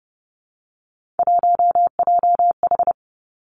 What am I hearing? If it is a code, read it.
Morse code, heard as a keyed tone, about 30 wpm, 700 Hz. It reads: 1JH
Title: Three Challenging Letter-Number Combinations 30wpm